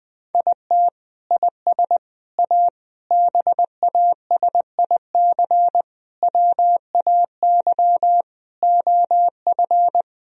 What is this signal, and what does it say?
Morse code, heard as a keyed tone, about 20 wpm, 700 Hz. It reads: IT IS A BASIC WAY OF
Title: Sentences from Top 1000 Words - Review Part 3 - 20wpm